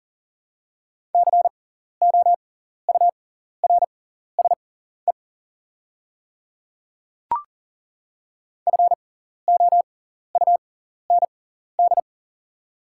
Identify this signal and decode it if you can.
Morse code, heard as a keyed tone, about 40 wpm, spaced as for 14 wpm, 700 Hz. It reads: COURSE FOUND
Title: 6x Character Spacing - Top 500 Words - 40wpm